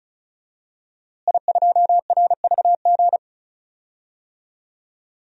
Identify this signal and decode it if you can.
Morse code, heard as a keyed tone, about 36 wpm, 700 Hz. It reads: I2RVG
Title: DXCC-100 Call Signs - 5 Characters - Repeated 3x - 35wpm